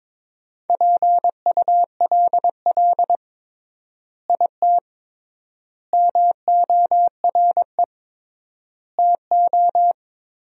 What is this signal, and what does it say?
Morse code, heard as a keyed tone, about 22 wpm, 700 Hz. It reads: PULL IT MORE TO
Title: 3x Word-Spacing - Sentences from Top 400 Words - 22wpm